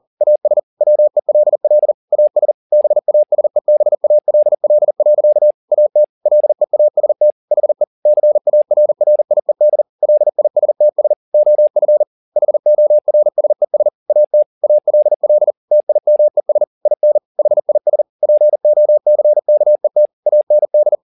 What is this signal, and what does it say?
Morse code, heard as a keyed tone, about 40 wpm, 600 Hz. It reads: AS WELL AS BASEBALL. AT LEAST HE CARRIED LISTS OF HORSES AT ALL TIMES IN HIS POCKET AND